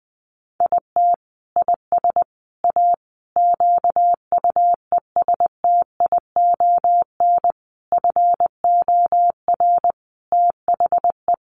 Morse code, heard as a keyed tone, about 20 wpm, 700 Hz. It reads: IT IS A QUESTION FOR THE
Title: Sentences from Top 1300 Words 20wpm